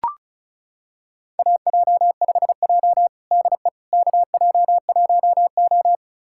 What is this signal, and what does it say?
Morse code, heard as a keyed tone, about 36 wpm, 700 Hz. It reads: AJ5J DE KJ1O
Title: USA Callsign DE Callsign - 4 Characters - 35wpm